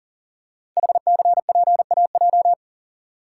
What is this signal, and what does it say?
Morse code, heard as a keyed tone, about 43 wpm, 700 Hz. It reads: HCPAJ
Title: Code Groups - 5 Letters - 40wpm